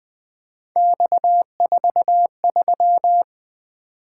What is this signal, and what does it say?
Morse code, heard as a keyed tone, about 20 wpm, 700 Hz. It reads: X43
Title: Three Letter-Number Combinations - 20wpm